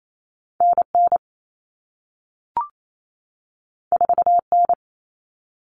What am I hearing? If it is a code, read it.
Morse code, heard as a keyed tone, about 28 wpm, 700 Hz. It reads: NN 4N
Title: All Cut Numbers - 2-Digit 28wpm